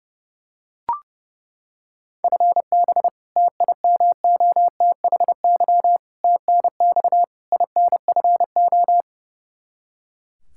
Morse code, heard as a keyed tone, about 32 wpm, 700 Hz. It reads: FB TIMOTHY TNX INFO
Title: K1USN SST QSO Practice 30wpm